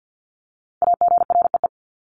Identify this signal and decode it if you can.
Morse code, heard as a keyed tone, about 50 wpm, 700 Hz. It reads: AGREE
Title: Top 1001-1250 Words 50wpm